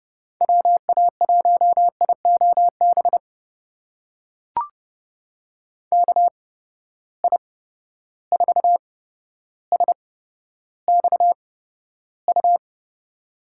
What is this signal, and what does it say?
Morse code, heard as a keyed tone, about 30 wpm, spaced as for 13 wpm, 700 Hz. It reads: WA1IOB KI4SXU
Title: USA Call Signs - 6 Characters - 8x Character Spacing - 30wpm